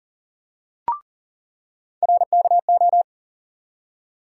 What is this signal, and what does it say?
Morse code, heard as a keyed tone, about 37 wpm, 700 Hz. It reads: RKO
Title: USA Call Sign Postfixes - 3 Letters 40wpm